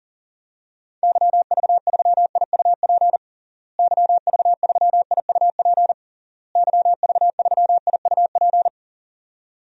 Morse code, heard as a keyed tone, about 40 wpm, 700 Hz. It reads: YV3IUP YV3IUP YV3IUP
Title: DXCC-100 Call Signs - 6 Characters - Repeated 3x - 40wpm